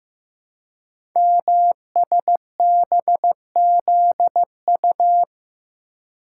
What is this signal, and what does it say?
Morse code, heard as a keyed tone, about 15 wpm, 700 Hz. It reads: MSBZU
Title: Code Groups - 5 Letters - 15wpm